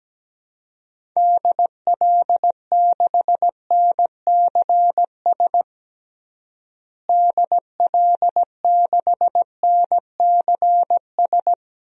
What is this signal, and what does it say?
Morse code, heard as a keyed tone, about 17 wpm, 700 Hz. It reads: DL6NCS DL6NCS
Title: DXCC-100 Call Signs - 6 Characters - Repeated 2x - 17wpm